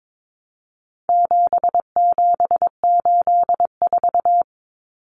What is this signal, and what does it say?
Morse code, heard as a keyed tone, about 22 wpm, 700 Hz. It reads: 7784
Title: Common Cut Numbers - 4-Digit 22wpm